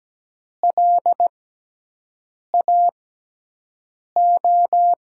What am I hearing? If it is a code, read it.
Morse code, heard as a keyed tone, about 17 wpm, spaced as for 6 wpm, 700 Hz. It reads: LAO
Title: USA Call Signs - 6 Characters - 6x Character Spacing - 17wpm